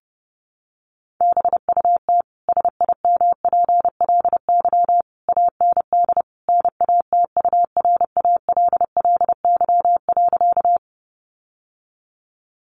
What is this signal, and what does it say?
Morse code, heard as a keyed tone, about 28 wpm, 700 Hz. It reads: BUT SIMPLY AND NATURALLY.
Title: Sentences from Top 1300 Words 30wpm